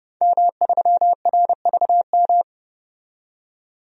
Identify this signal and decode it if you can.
Morse code, heard as a keyed tone, about 31 wpm, 690 Hz. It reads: M3RVM
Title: DXCC-100 Call Signs - 5 Characters - 30wpm